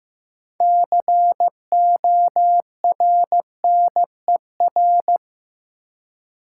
Morse code, heard as a keyed tone, about 15 wpm, 700 Hz. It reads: CORNER
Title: Six Letter Words - 15wpm